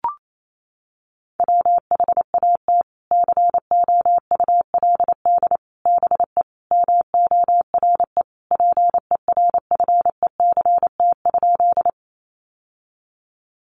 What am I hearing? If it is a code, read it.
Morse code, heard as a keyed tone, about 28 wpm, 700 Hz. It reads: WHAT COULD BE MORE PERFECT?
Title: Sentences from Top 1000 Words 28wpm